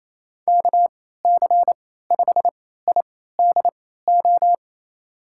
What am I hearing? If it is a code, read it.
Morse code, heard as a keyed tone, about 28 wpm, spaced as for 16 wpm, 700 Hz. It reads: KC5IDO
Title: USA Call Signs - 6 Characters - 3x Character Spacing - 28wpm